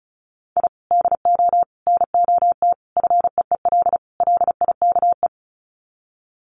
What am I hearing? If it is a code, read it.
Morse code, heard as a keyed tone, about 34 wpm, 700 Hz. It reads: I DO NOT FEEL LIKE
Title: Sets of 5 Words Encore 35wpm